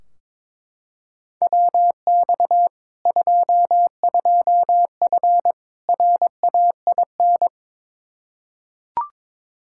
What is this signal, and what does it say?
Morse code, heard as a keyed tone, about 22 wpm, 700 Hz. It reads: WX 22F RAIN
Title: Practice QSO - Exchange 2 - Info - 22wpm